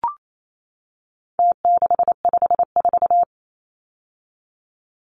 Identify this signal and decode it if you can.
Morse code, heard as a keyed tone, about 28 wpm, 700 Hz. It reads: T654
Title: Numbers - Contest Style - 4-Digit 28wpm